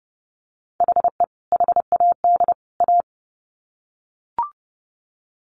Morse code, heard as a keyed tone, about 30 wpm, 700 Hz. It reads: HE HAD A
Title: Sets of 3 Words 30wpm